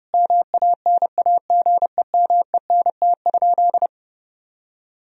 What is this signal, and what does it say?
Morse code, heard as a keyed tone, about 32 wpm, 700 Hz. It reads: MANAGEMENT?
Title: Sentences from Top 1000 Words - Review Part 4 - 30wpm